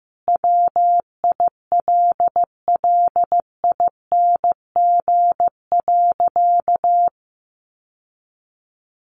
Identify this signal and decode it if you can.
Morse code, heard as a keyed tone, about 15 wpm, 700 Hz. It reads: WILLING.
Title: Sentences from Top 1000 Words 15wpm